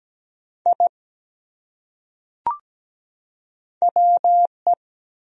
Morse code, heard as a keyed tone, about 17 wpm, 700 Hz. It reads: I WE
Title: Top 100 Words 17wpm